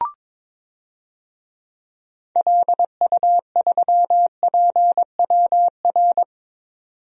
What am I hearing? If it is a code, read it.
Morse code, heard as a keyed tone, about 22 wpm, 700 Hz. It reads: LU3PWR
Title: DXCC-100 Call Signs - 6 Characters - 22wpm